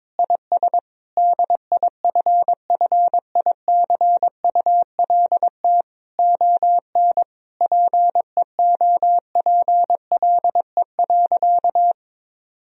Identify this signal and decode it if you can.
Morse code, heard as a keyed tone, about 22 wpm, 700 Hz. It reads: IS DIFFICULT ON PEOPLE.
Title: Sentences from Top 1000 Words - Review Part 3 - 22wpm